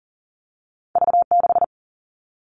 Morse code, heard as a keyed tone, about 41 wpm, 700 Hz. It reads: V6
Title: All Cut Numbers - 2-Digit 40wpm